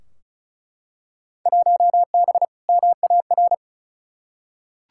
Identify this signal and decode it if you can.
Morse code, heard as a keyed tone, about 36 wpm, 700 Hz. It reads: 1B MAR
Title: Field Day Exchanges Encore - 35wpm